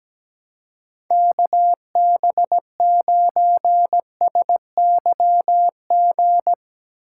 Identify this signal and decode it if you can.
Morse code, heard as a keyed tone, about 17 wpm, 700 Hz. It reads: KB9SYG
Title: USA Call Signs - 6 Characters - 3x Character Spacing - 17wpm